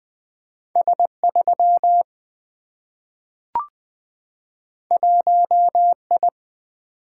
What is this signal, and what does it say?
Morse code, heard as a keyed tone, about 20 wpm, 700 Hz. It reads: S 3 1 I
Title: Single-Letter Single-Number 20wpm